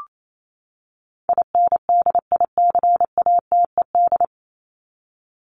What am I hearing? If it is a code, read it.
Morse code, heard as a keyed tone, about 28 wpm, 700 Hz. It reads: INDICATED
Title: ED Suffix - Top 100 Words with Base Words 28wpm